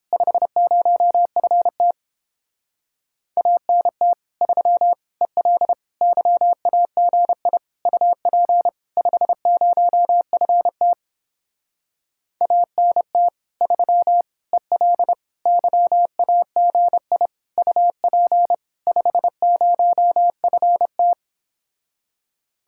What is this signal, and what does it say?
Morse code, heard as a keyed tone, about 29 wpm, 700 Hz. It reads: 50FT ANT 3 EL YAGI UP 50FT ANT 3 EL YAGI UP 50FT